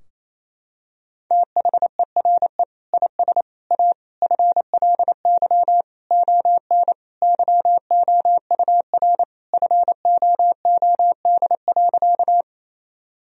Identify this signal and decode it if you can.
Morse code, heard as a keyed tone, about 27 wpm, 700 Hz. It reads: THERE IS A FLY ON YOUR FOOD.
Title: Sentences from Top 1000 Words - Review Part 2 - 28wpm